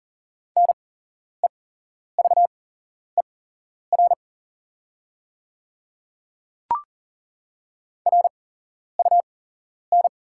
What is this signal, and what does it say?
Morse code, heard as a keyed tone, about 40 wpm, spaced as for 11 wpm, 690 Hz. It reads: NEVER RUN
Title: 8x Character Spacing - Top 500 Words - 40wpm